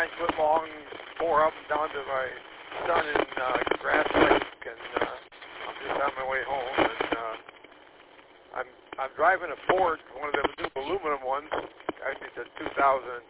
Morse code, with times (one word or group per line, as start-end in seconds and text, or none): none